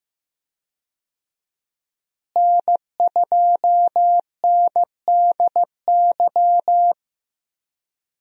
Morse code, heard as a keyed tone, about 15 wpm, 700 Hz. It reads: N2NDY
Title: USA Call Signs - 5 Characters - Repeated 2x - 15wpm